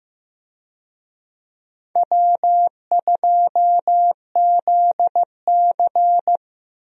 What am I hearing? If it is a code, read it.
Morse code, heard as a keyed tone, about 15 wpm, 700 Hz. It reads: W2ZC
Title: USA Call Signs - 4 Characters - 15wpm